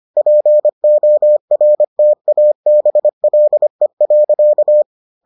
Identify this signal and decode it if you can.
Morse code, heard as a keyed tone, about 25 wpm, 600 Hz. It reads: PORTABLE.